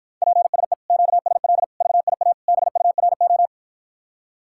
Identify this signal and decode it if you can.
Morse code, heard as a keyed tone, about 52 wpm, 700 Hz. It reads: PSE QSL VIA BURO